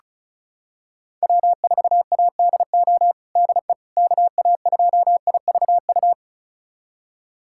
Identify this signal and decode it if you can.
Morse code, heard as a keyed tone, about 36 wpm, 700 Hz. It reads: W4ADO DE KA2IVU
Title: USA Callsign DE Callsign 35wpm